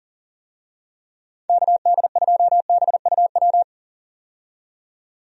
Morse code, heard as a keyed tone, about 40 wpm, 700 Hz. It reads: KD2BUW
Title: USA Call Signs - 6 Characters - 40wpm